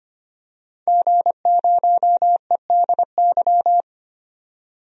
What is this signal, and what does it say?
Morse code, heard as a keyed tone, about 25 wpm, 700 Hz. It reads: G0EDY
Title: DXCC-100 Call Signs - 5 Characters - 25wpm